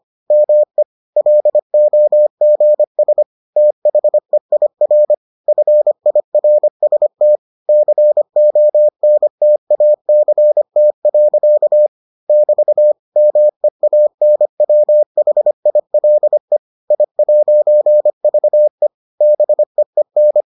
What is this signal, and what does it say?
Morse code, heard as a keyed tone, about 25 wpm, 600 Hz. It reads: ME LOGS THEIR FIRST CONTACT. <BT> MEANWHILE I'VE BEEN